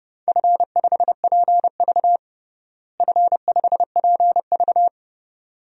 Morse code, heard as a keyed tone, about 30 wpm, 700 Hz. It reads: F5PV F5PV
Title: DXCC-100 Call Signs - 4 Characters - Repeated 2x - 30wpm